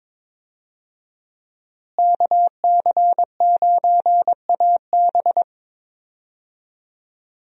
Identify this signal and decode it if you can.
Morse code, heard as a keyed tone, about 22 wpm, 700 Hz. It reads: KC9AB
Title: USA Call Signs - 5 Characters - Repeated 3x - 22wpm